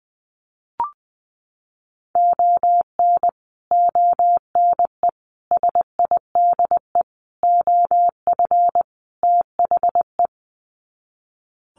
Morse code, heard as a keyed tone, about 20 wpm, 700 Hz. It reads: ON ONE SIDE OF THE
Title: Sets of 5 Words 20wpm